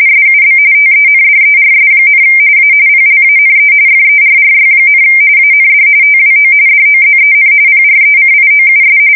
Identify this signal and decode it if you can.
RTTY, about 45.45 baud. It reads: SUB TENDERS, A PLANE FERRY, AND TWO DOZEN PRECIOUS
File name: fsk170 / USS-New-Jersey